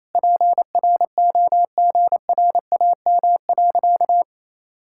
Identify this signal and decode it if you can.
Morse code, heard as a keyed tone, about 27 wpm, 700 Hz. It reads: PROGRAM.
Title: Sentences from Top 1000 Words - Review Part 5 - 28wpm